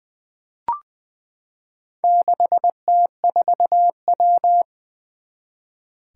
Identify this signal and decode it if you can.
Morse code, heard as a keyed tone, about 20 wpm, 700 Hz. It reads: 6T4W
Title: Four Challenging Letter-Number Combinations 20wpm